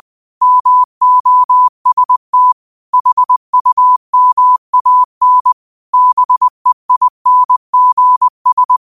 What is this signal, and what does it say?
Morse code, heard as a keyed tone, about 20 wpm, 1000 Hz. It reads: MOST HUMAN BEINGS
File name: qotdfiles / 20wpm